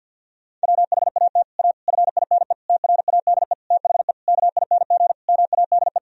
Morse code, heard as a keyed tone, about 52 wpm, 700 Hz. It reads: WHAT A FINE TRADE THE KING MADE
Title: Sentences from Top 1000 Words - Review Part 4 - 50wpm